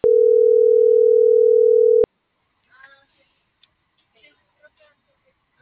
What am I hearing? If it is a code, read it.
Ringback tone (North America), 440 + 480 Hz.